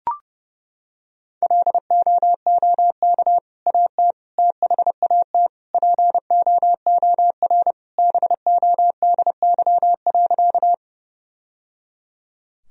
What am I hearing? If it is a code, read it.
Morse code, heard as a keyed tone, about 30 wpm, 700 Hz. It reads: LOOK AT THAT POOR BODY.
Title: Sentences from Top 1000 Words - Review Part 2 - 30wpm